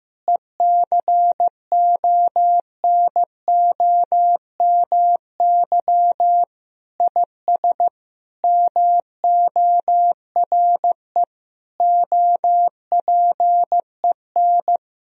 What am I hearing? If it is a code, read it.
Morse code, heard as a keyed tone, about 15 wpm, 700 Hz. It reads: ECONOMY IS MORE OPEN
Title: Sentences from Top 1000 Words - Review Part 3 - 15wpm